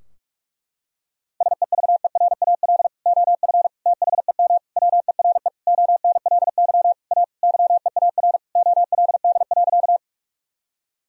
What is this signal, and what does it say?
Morse code, heard as a keyed tone, about 41 wpm, 700 Hz. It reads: SEVERAL OF THEM WERE ONLY A YEAR OLD.
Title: Sentences from Top 1000 Words - Review Part 2 - 45wpm